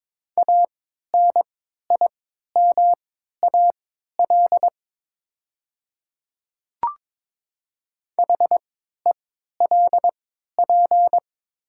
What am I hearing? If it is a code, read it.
Morse code, heard as a keyed tone, about 23 wpm, spaced as for 13 wpm, 700 Hz. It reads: ANIMAL HELP
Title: 3x Character Spacing - Top 200 Words - 22wpm